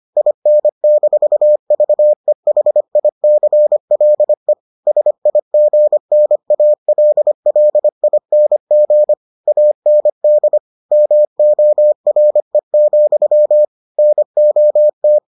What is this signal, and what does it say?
Morse code, heard as a keyed tone, about 25 wpm, 600 Hz. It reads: IN-VEHICLE SIGNALLING AND MORE, NOT